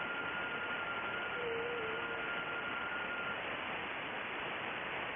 FT8.
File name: contests / sept-vhf-k5tr- -2200